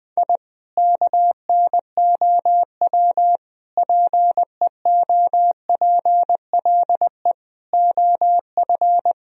Morse code, heard as a keyed tone, about 20 wpm, 700 Hz. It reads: I KNOW PEOPLE OF
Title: Sentences from Top 1000 Words - Review Part 4 - 20wpm